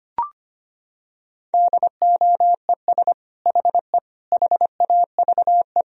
Morse code, heard as a keyed tone, about 25 wpm, 700 Hz. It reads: DOES HE HAVE
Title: Sentences from Top 1000 Words - Review Part 3 - 25wpm